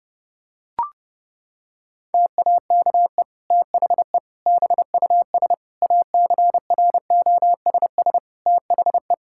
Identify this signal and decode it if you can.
Morse code, heard as a keyed tone, about 30 wpm, 700 Hz. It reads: TAKE THE BUS ACROSS THE
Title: Sentences from Top 1000 Words - Review Part 6 - 30wpm